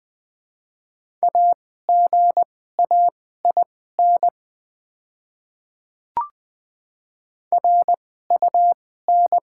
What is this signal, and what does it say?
Morse code, heard as a keyed tone, about 20 wpm, spaced as for 14 wpm, 700 Hz. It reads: AGAIN RUN
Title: 2x Character Spacing - Top 300 Words - 20wpm